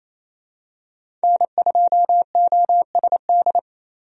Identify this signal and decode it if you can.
Morse code, heard as a keyed tone, about 28 wpm, 700 Hz. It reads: N2OSD
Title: USA Call Signs - 5 Characters - 28wpm